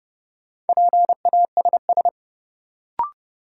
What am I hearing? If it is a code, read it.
Morse code, heard as a keyed tone, about 30 wpm, 700 Hz. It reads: PASS
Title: Sentences from Top 400 Words - New Word Drill - 30wpm